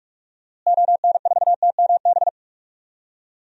Morse code, heard as a keyed tone, about 44 wpm, 700 Hz. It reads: ON4TMB
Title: DXCC-100 Call Signs - 6 Characters - Repeated 2x - 45wpm